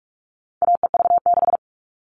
Morse code, heard as a keyed tone, about 44 wpm, 700 Hz. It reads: AEV6
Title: All Cut Numbers - 4-Digit 45wpm